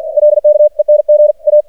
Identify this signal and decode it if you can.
Morse code, heard as a keyed tone, about 52 wpm, 600 Hz. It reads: LK ENG A
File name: archived / CRSnet